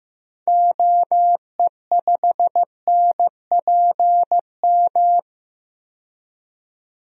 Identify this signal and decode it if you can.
Morse code, heard as a keyed tone, about 15 wpm, 700 Hz. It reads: OE5NPM
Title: DXCC-100 Call Signs - 6 Characters - 15wpm